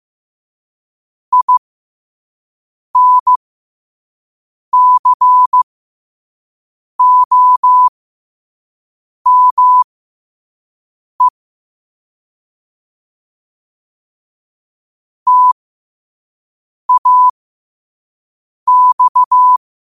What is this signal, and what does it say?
Morse code, heard as a keyed tone, about 15 wpm, spaced as for 5 wpm, 1000 Hz. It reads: INCOME TAX
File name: qotdfiles / 5wpm